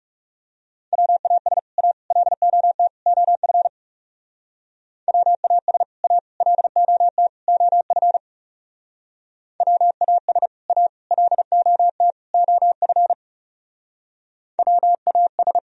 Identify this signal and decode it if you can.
Morse code, heard as a keyed tone, about 39 wpm, 700 Hz. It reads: WAS A LOT OF WAS A LOT OF WAS A LOT OF WAS